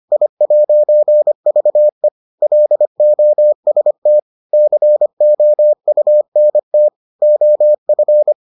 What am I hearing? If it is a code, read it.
Morse code, heard as a keyed tone, about 25 wpm, 600 Hz. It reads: I'VE LOST COUNT OF